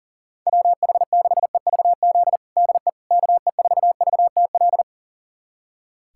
Morse code, heard as a keyed tone, about 40 wpm, 700 Hz. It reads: WH6EVZ DE KE4VTL
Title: USA Callsign DE Callsign - 6 Characters - 40wpm